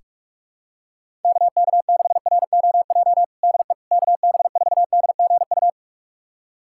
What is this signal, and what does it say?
Morse code, heard as a keyed tone, about 42 wpm, 700 Hz. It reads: KK6ROJ DE KB4DGU
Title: USA Callsign DE Callsign - 6 Characters - 45wpm